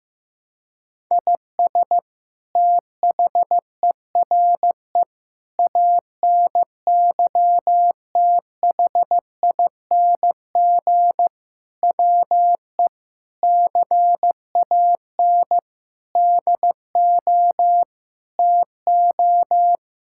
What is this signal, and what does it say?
Morse code, heard as a keyed tone, about 15 wpm, 700 Hz. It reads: IS THERE ANYTHING WE CAN DO TO
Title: Sentences from Top 1000 Words - Review Part 6 - 15wpm